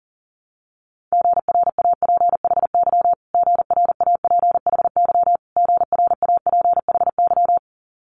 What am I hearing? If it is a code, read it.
Morse code, heard as a keyed tone, about 43 wpm, 700 Hz. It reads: GRAPHY GRAPHY GRAPHY
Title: Multi-letter Phonemes Repeated 3x 40wpm